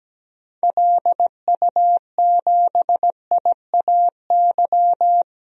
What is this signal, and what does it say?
Morse code, heard as a keyed tone, about 17 wpm, 700 Hz. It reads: LU7IAY
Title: DXCC-100 Call Signs - 6 Characters - 17wpm